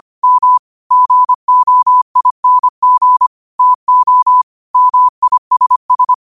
Morse code, heard as a keyed tone, about 25 wpm, 1000 Hz. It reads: M GOING TO MISS
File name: 25wpm